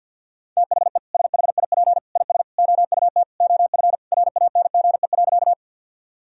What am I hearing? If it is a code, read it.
Morse code, heard as a keyed tone, about 50 wpm, 700 Hz. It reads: THE SHIP IS OUT OF RANGE.
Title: Sentences from Top 1000 Words - Review Part 4 - 50wpm